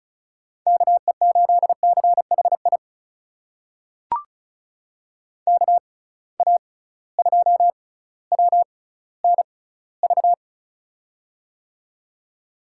Morse code, heard as a keyed tone, about 36 wpm, spaced as for 18 wpm, 690 Hz. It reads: KE8CHI KA2WNV
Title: USA Call Signs - 6 Characters - 6x Character Spacing - 35wpm